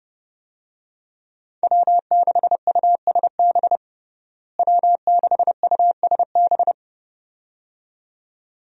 Morse code, heard as a keyed tone, about 32 wpm, 700 Hz. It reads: W6USB W6USB
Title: USA Call Signs - 5 Characters - Repeated 2x - 30wpm